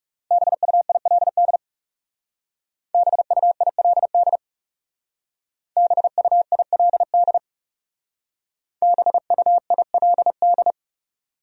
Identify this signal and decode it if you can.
Morse code, heard as a keyed tone, about 38 wpm, 700 Hz. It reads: BUILD BUILD BUILD BUILD